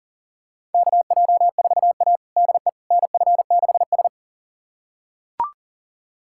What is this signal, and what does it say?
Morse code, heard as a keyed tone, about 40 wpm, 700 Hz. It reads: KJ4A DE NF6S
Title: USA Callsign DE Callsign - 4 Characters - 40wpm